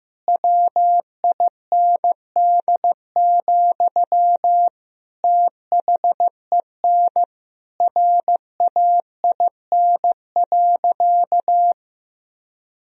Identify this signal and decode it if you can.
Morse code, heard as a keyed tone, about 15 wpm, 700 Hz. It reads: WIND, THEN RAIN.
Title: Sentences from Top 1000 Words - Review Part 2 - 15wpm